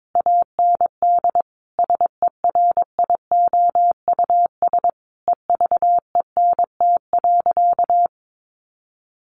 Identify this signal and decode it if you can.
Morse code, heard as a keyed tone, about 22 wpm, 700 Hz. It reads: AND SERIOUS EVENT.
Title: Sentences from Top 1400 Words 22wpm